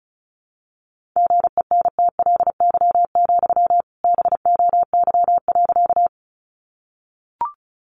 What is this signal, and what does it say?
Morse code, heard as a keyed tone, about 34 wpm, 700 Hz. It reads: GENTLY, BOY.
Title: Sentences from Top 1500 Words 35wpm